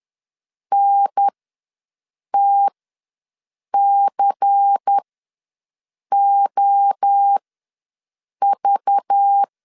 Morse code, heard as a keyed tone, about 10 wpm, spaced as for 6 wpm, 800 Hz. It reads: NTCOV
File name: LEZIONI / 7 GF9Z / 7 GF9 ws 10wpm 4034